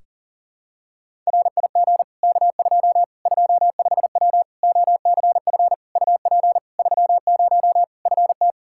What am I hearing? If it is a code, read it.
Morse code, heard as a keyed tone, about 40 wpm, 700 Hz. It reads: RIG K2 25W OCF UP 30 FT
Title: Practice QSO - Exchange 2 - Info - 40wpm